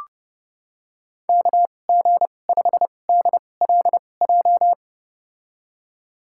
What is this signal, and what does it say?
Morse code, heard as a keyed tone, about 31 wpm, spaced as for 22 wpm, 700 Hz. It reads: KG5DLJ
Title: USA Call Signs - 2x Character Spacing - 30wpm